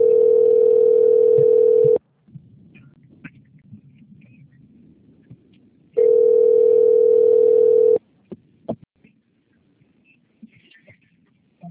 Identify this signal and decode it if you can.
Ringback tone (North America), 440 + 480 Hz.
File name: out- - - - - .19530